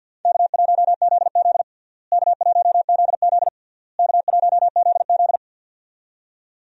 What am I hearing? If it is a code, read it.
Morse code, heard as a keyed tone, about 50 wpm, 700 Hz. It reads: K1ZZ K1ZZ K1ZZ
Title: USA Call Signs - 4 Characters - Repeated 3x - 50wpm